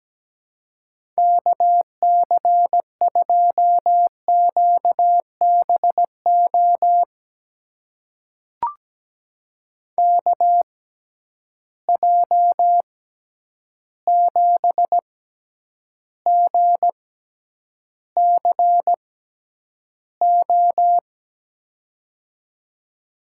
Morse code, heard as a keyed tone, about 17 wpm, spaced as for 9 wpm, 700 Hz. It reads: KC2QBO KJ7GCO
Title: USA Call Signs - 6 Characters - 6x Character Spacing - 17wpm